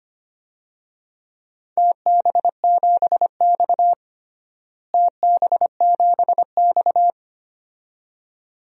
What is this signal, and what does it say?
Morse code, heard as a keyed tone, about 25 wpm, 700 Hz. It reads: TB7X TB7X
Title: DXCC-100 Call Signs - 4 Characters - Repeated 2x - 25wpm